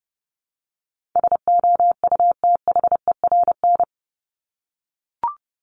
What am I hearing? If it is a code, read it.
Morse code, heard as a keyed tone, about 30 wpm, 700 Hz. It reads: SOUTHERN
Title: Top 1751-2000 Words 30wpm